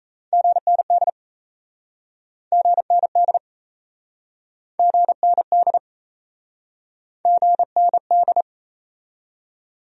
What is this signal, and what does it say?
Morse code, heard as a keyed tone, about 37 wpm, 700 Hz. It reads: GND GND GND GND